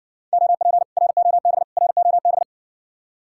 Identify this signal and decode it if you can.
Morse code, heard as a keyed tone, about 60 wpm, 700 Hz. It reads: OP ROB ROB